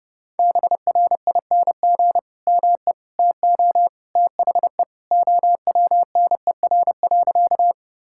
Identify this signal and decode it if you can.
Morse code, heard as a keyed tone, about 30 wpm, 700 Hz. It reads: BRING ME TO THE OWNER.
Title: Sentences from Top 1600 Words 30wpm